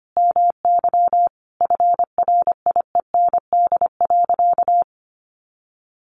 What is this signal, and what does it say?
Morse code, heard as a keyed tone, about 25 wpm, 700 Hz. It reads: MY FRIEND.
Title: Sentences from Top 1300 Words 25wpm